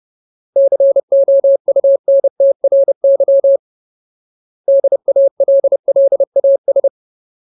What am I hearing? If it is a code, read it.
Morse code, heard as a keyed tone, about 30 wpm, 550 Hz. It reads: COUNTRY DALLAS
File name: Ultimate Word List / 30WPM Ultimate